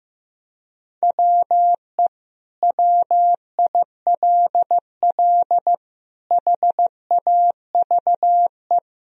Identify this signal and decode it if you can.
Morse code, heard as a keyed tone, about 15 wpm, 700 Hz. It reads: WE WILL HAVE
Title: Sentences from Top 1000 Words - Review Part 1 - 15wpm